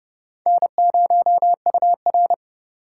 Morse code, heard as a keyed tone, about 38 wpm, 700 Hz. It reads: N0UR